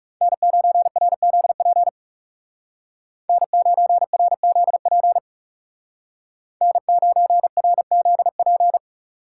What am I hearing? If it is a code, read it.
Morse code, heard as a keyed tone, about 37 wpm, 700 Hz. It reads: N9RZP N9RZP N9RZP
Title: Speed Racing - USA 1x3 Call Signs 30wpm